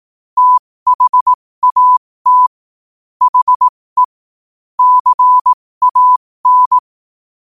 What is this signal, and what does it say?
Morse code, heard as a keyed tone, about 18 wpm, spaced as for 15 wpm, 1000 Hz. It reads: THAT HE CAN
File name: qotdfiles / 15wpm